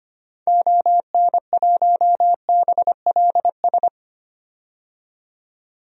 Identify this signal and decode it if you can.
Morse code, heard as a keyed tone, about 25 wpm, 700 Hz. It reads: ON1BLS
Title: DXCC-100 Call Signs - 6 Characters - 25wpm